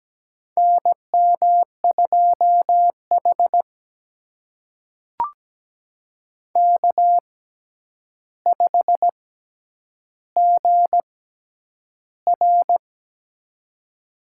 Morse code, heard as a keyed tone, about 17 wpm, spaced as for 9 wpm, 700 Hz. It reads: NM2H K5GR
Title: USA Call Signs - 4 Characters - 6x Character Spacing - 17wpm